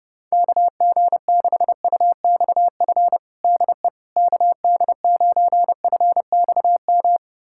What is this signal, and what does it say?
Morse code, heard as a keyed tone, about 30 wpm, 700 Hz. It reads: KG6UXF DE KD9FXM
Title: USA Callsign DE Callsign 30wpm